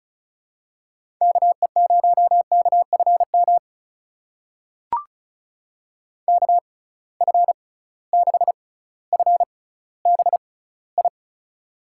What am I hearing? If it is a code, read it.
Morse code, heard as a keyed tone, about 36 wpm, spaced as for 18 wpm, 700 Hz. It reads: KE0KFM KF6FBI
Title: USA Call Signs - 6 Characters - 6x Character Spacing - 35wpm